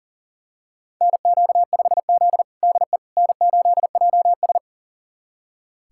Morse code, heard as a keyed tone, about 40 wpm, 700 Hz. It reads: NQ5Z DE N8JS
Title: USA Callsign DE Callsign - 4 Characters - 40wpm